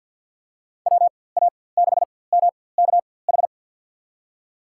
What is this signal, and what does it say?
Morse code, heard as a keyed tone, about 52 wpm, spaced as for 23 wpm, 700 Hz. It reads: WA6MKH
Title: USA Call Signs - 6 Characters - 4x Character Spacing - 50wpm